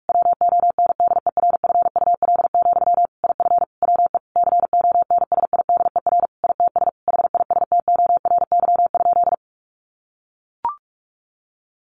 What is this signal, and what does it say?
Morse code, heard as a keyed tone, about 44 wpm, 700 Hz. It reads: WONDERFUL, IF WE CONSIDER ITS HISTORY?
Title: Sentences from Top 1100 Words 45wpm